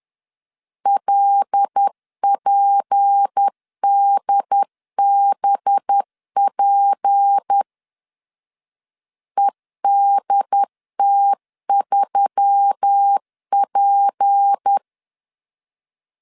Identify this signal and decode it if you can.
Morse code, heard as a keyed tone, about 10 wpm, 800 Hz. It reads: LPDBP EDT3P
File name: LEZIONI / 6 PKX0 / 6 P 10wpm 2432